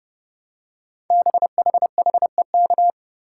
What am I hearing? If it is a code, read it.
Morse code, heard as a keyed tone, about 30 wpm, 700 Hz. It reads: BHHEK
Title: Code Groups - 5 Letters - 30wpm